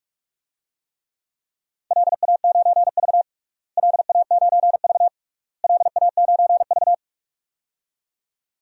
Morse code, heard as a keyed tone, about 42 wpm, 700 Hz. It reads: LA9V LA9V LA9V
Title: DXCC-100 Call Signs - 4 Characters - Repeated 3x - 45wpm